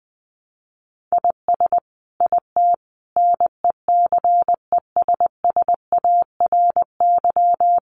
Morse code, heard as a keyed tone, about 20 wpm, 700 Hz. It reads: IS IT NECESSARY
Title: Sentences from Top 1500 Words 20wpm